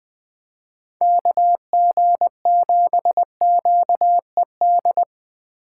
Morse code, heard as a keyed tone, about 23 wpm, 700 Hz. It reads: KG7QED
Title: Speed Racing - USA 2x3 Call Signs 20wpm